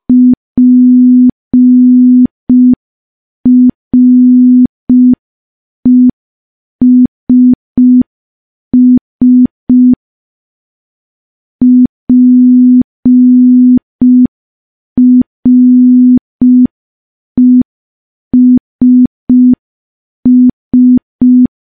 Morse code, heard as a keyed tone, about 5 wpm, 250 Hz. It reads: PRESS PRESS
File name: morse / morse-training-05WPM